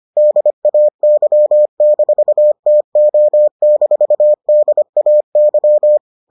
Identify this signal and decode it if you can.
Morse code, heard as a keyed tone, about 25 wpm, 600 Hz. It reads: DAY-TO-DAY